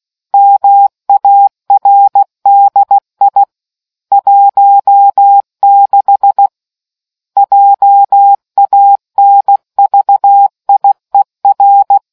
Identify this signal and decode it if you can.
Morse code, heard as a keyed tone, about 16 wpm, 800 Hz. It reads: MARDI 16 JANVIER